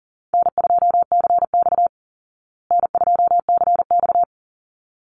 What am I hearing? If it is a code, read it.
Morse code, heard as a keyed tone, about 40 wpm, 700 Hz. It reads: N2CX N2CX
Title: Call Signs Repeated 3x - 40wpm